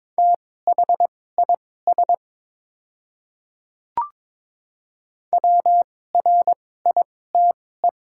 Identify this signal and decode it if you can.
Morse code, heard as a keyed tone, about 22 wpm, spaced as for 16 wpm, 700 Hz. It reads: THIS WRITE
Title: 2x Character Spacing - Top 100 Words - 22wpm